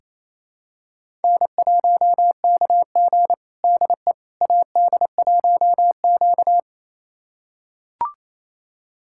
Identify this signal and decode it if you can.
Morse code, heard as a keyed tone, about 28 wpm, 700 Hz. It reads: N1KG DE AD1Q
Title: USA Callsign DE Callsign - 4 Characters - 28wpm